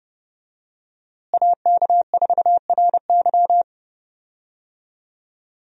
Morse code, heard as a keyed tone, about 30 wpm, 700 Hz. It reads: AK4RY
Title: USA Call Signs - 5 Characters - Repeated 3x - 30wpm